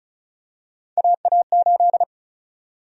Morse code, heard as a keyed tone, about 34 wpm, 700 Hz. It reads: AA8
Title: USA Call Sign Prefixes - 35wpm